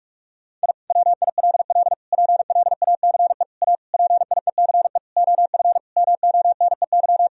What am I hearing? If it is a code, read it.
Morse code, heard as a keyed tone, about 41 wpm, 700 Hz. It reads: I WILL PLACE A PIECE OF MONEY
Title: Sentences from Top 1000 Words - Review Part 2 - 45wpm